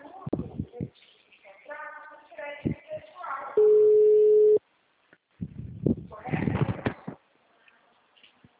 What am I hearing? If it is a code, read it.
Ringback tone (Europe, 425 Hz), with speech.